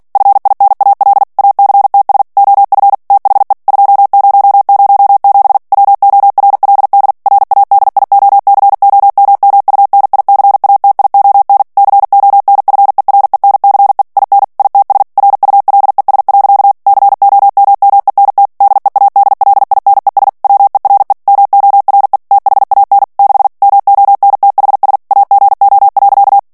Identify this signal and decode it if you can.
Morse code, heard as a keyed tone, about 48 wpm, 800 Hz. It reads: FINAL ACTS OF THE 2007 WORLD RADIOCOMMUNICATION CONFERENCE IN ITS RULES. COMMENT DEADLINES WERE MORE THAN 6 MONTHS AGO.
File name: ANGLAIS texte clair 3 vitesse 45 mots minute